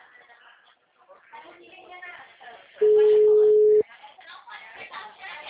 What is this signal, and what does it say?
Ringback tone (Europe, 425 Hz), with speech.
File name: out- -2012- - - .65624